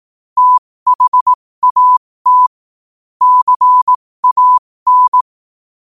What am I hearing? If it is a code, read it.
Morse code, heard as a keyed tone, about 18 wpm, spaced as for 15 wpm, 1000 Hz. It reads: THAT CAN
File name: qotdfiles / 15wpm